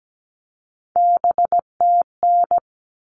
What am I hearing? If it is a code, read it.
Morse code, heard as a keyed tone, about 17 wpm, 700 Hz. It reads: BTN
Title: All Cut Numbers - 3-Digit 17wpm